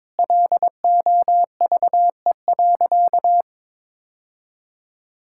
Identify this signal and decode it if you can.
Morse code, heard as a keyed tone, about 22 wpm, 700 Hz. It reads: LOVE.
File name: practice / 22wpm / Sentences with 4-Word Phrases - 22wpm